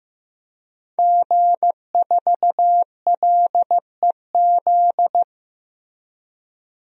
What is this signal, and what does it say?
Morse code, heard as a keyed tone, about 15 wpm, 700 Hz. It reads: G4LEZ
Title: DXCC-100 Call Signs - 5 Characters - Repeated 2x - 15wpm